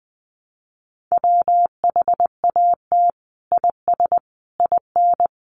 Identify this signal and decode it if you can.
Morse code, heard as a keyed tone, about 20 wpm, 700 Hz. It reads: WHAT IS IN
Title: Sentences from Top 200 Words 20wpm